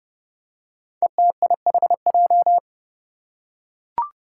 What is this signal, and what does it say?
Morse code, heard as a keyed tone, about 30 wpm, 700 Hz. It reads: ETIHJ
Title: Five Challenging Letter-Number Combinations 30wpm